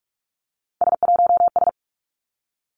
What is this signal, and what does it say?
Morse code, heard as a keyed tone, about 46 wpm, 700 Hz. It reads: S1S
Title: Three Challenging Letter-Number Combinations 45wpm